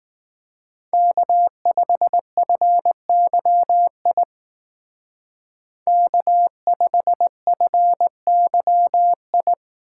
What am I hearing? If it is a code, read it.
Morse code, heard as a keyed tone, about 19 wpm, 700 Hz. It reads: K5FYI K5FYI
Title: Speed Racing - USA 1x3 Call Signs 15wpm